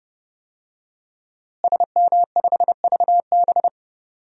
Morse code, heard as a keyed tone, about 32 wpm, 700 Hz. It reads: SM5VB
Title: DXCC-100 Call Signs - 5 Characters - Repeated 3x - 30wpm